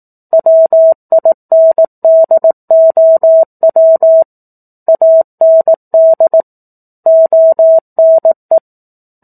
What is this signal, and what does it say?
Morse code, heard as a keyed tone, about 18 wpm, 650 Hz. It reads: WINDOW AND ONE
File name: 04-Chapter-04 18 WPM